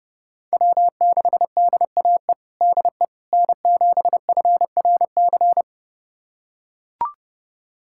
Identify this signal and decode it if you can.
Morse code, heard as a keyed tone, about 30 wpm, 700 Hz. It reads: W6DAE DE N7FRC
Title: USA Callsign DE Callsign - 5 Characters - 30wpm